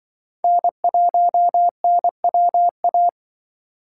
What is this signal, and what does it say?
Morse code, heard as a keyed tone, about 25 wpm, 700 Hz. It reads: N1NWA